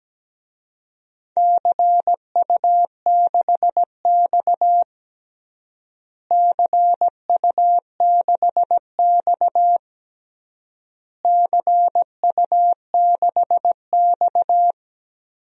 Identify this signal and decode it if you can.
Morse code, heard as a keyed tone, about 17 wpm, 700 Hz. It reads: CU6X CU6X CU6X
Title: DXCC-100 Call Signs - 4 Characters - Repeated 3x - 17wpm